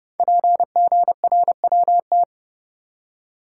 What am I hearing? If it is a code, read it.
Morse code, heard as a keyed tone, about 30 wpm, 700 Hz. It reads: PGRWT
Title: Code Groups - 5 Letters - 30wpm